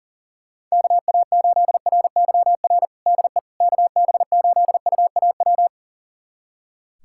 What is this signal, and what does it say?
Morse code, heard as a keyed tone, about 37 wpm, 700 Hz. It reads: KA8RYR DE KB8UAW
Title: USA Callsign DE Callsign 40wpm